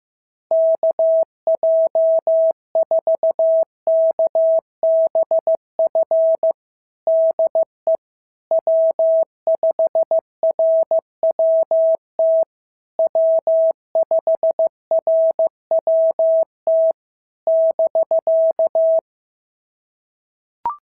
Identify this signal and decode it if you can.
Morse code, heard as a keyed tone, about 14 wpm, 650 Hz. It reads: KJ4KBF DE W5RWT W5RWT <BK>
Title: SOTA QSO Practice 15wpm